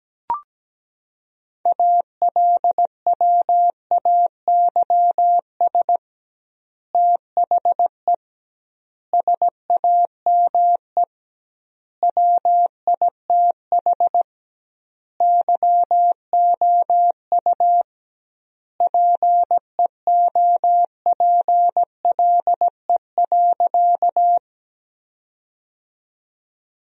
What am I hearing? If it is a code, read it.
Morse code, heard as a keyed tone, about 17 wpm, 700 Hz. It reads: ALWAYS THE SAME WITH YOU PEOPLE.
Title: 2x Word-Spacing - Sentences from Top 300 Words - 17wpm